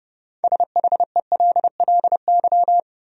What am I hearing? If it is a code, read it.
Morse code, heard as a keyed tone, about 30 wpm, 700 Hz. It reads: SHELLY
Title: USA Common Names - 30wpm